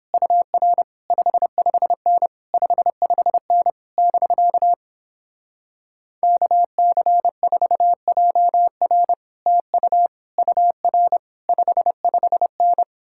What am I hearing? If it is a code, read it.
Morse code, heard as a keyed tone, about 29 wpm, 700 Hz. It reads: UR 55N 55N <BK> KC4JR TU UR 55N